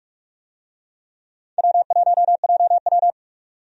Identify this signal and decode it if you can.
Morse code, heard as a keyed tone, about 43 wpm, 700 Hz. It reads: W1JW
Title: USA Call Signs - 4 Characters - 45wpm